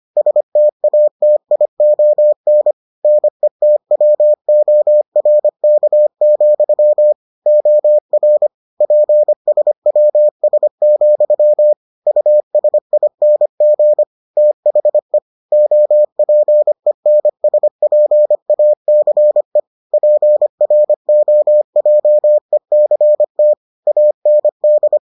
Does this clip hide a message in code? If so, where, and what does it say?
Yes: Morse code from 0.2 to 25.0 s, heard as a keyed tone, about 25 wpm, 600 Hz. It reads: STATION NETWORK, OR PSWS, USING THE OPENSPACE PROJECT AND